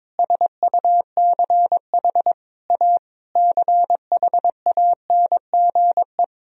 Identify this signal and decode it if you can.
Morse code, heard as a keyed tone, about 22 wpm, 700 Hz. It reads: SUCH A CHANGE
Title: Sentences from Top 1000 Words - Review Part 1 - 22wpm